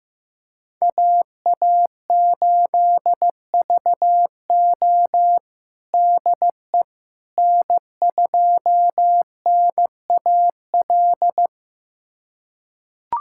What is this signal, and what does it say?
Morse code, heard as a keyed tone, about 16 wpm, 700 Hz. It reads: AA8VO DE N2NAL
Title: USA Callsign DE Callsign - 5 Characters - 15wpm